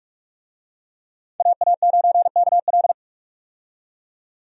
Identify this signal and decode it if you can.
Morse code, heard as a keyed tone, about 46 wpm, 700 Hz. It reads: AA9KL
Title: USA Call Signs - 5 Characters - 45wpm